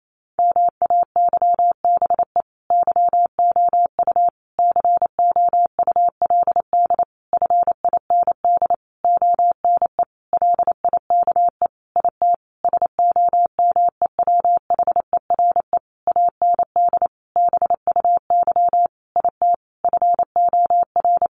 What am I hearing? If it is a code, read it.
Morse code, heard as a keyed tone, about 28 wpm, 700 Hz. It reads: MAYBE YOU COULD FIND ONE LIKE IT SOMEWHERE AND BUY IT FOR
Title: Sentences from Top 1400 Words 28wpm